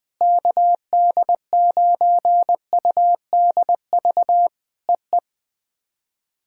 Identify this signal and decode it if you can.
Morse code, heard as a keyed tone, about 20 wpm, 700 Hz. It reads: KD9UDV EE
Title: POTA QSO Practice - 20wpm